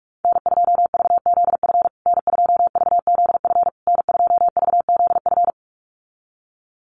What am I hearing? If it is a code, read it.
Morse code, heard as a keyed tone, about 44 wpm, 700 Hz. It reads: N2VZF N2VZF N2VZF
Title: USA Call Signs Repeated 3x 45wpm